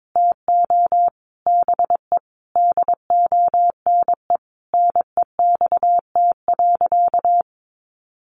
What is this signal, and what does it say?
Morse code, heard as a keyed tone, about 22 wpm, 700 Hz. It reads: TO BE DONE NEXT.
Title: Sentences from Top 1200 Words 22wpm